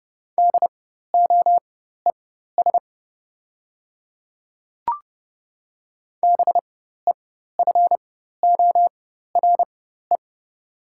Morse code, heard as a keyed tone, about 30 wpm, spaced as for 14 wpm, 700 Hz. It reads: DOES BEFORE
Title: 4x Character Spacing - Top 300 Words - 30wpm